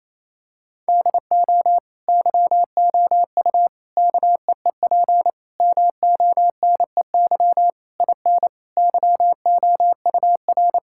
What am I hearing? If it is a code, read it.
Morse code, heard as a keyed tone, about 28 wpm, 700 Hz. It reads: DO YOU KEEP MONEY IN YOUR
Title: Sentences from Top 1000 Words - Review Part 6 - 28wpm